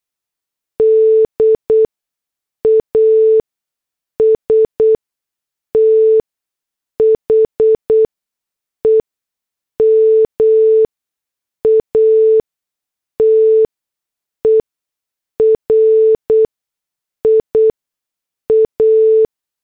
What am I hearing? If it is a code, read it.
Morse code, heard as a keyed tone, about 8 wpm, spaced as for 6 wpm, 440 Hz. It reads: D AS THE MATERIA
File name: morse / headlines